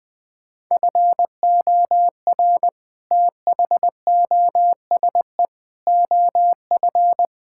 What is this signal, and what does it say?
Morse code, heard as a keyed tone, about 20 wpm, 700 Hz. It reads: FOR THOSE OF
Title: Sets of 6 Words - 20wpm